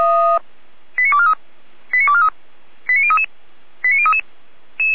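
Five-tone selective call (ZVEI), heard as quick runs of five tones, tones of about 70 ms. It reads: 89243 89243 89033 89033
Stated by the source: ZVEI five-tone code 89001 received by SDR at fire brigade Feuerwehr Orlamünde.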